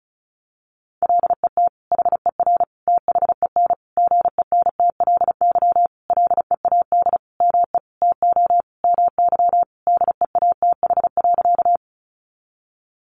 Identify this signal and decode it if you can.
Morse code, heard as a keyed tone, about 34 wpm, 700 Hz. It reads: LET HER THEN GENTLY LEAD ME TO MY DEATH.
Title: Sentences from Top 1500 Words 35wpm